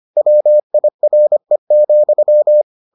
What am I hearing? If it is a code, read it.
Morse code, heard as a keyed tone, about 25 wpm, 600 Hz. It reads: WIRE,